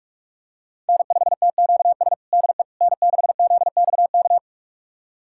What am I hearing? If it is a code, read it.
Morse code, heard as a keyed tone, about 46 wpm, 700 Hz. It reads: N5TQS DE N6ZXK
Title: USA Callsign DE Callsign - 5 Characters - 45wpm